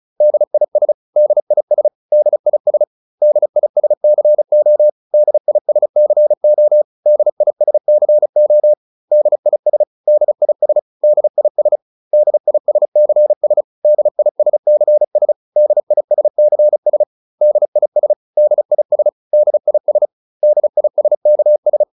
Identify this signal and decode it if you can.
Morse code, heard as a keyed tone, about 34 wpm, 600 Hz. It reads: DIS DIS DIS DISCO DISCO DISCO DIS DIS DIS DISCS DISCS DISCS DIS DIS DIS DISKS